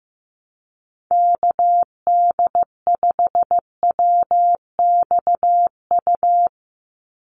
Morse code, heard as a keyed tone, about 15 wpm, 700 Hz. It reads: KD5WXU
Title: USA Call Signs 15wpm